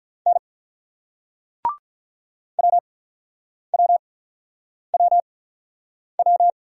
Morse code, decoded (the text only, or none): N WWWW